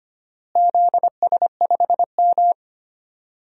Morse code, heard as a keyed tone, about 25 wpm, 700 Hz. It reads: ZS5M
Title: DXCC-100 Call Signs - 4 Characters - Repeated 2x - 25wpm